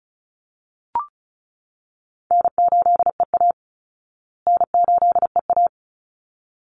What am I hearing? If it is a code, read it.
Morse code, heard as a keyed tone, about 34 wpm, 700 Hz. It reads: N8EA N8EA
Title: Easy Call Signs Repeated 2x - 35wpm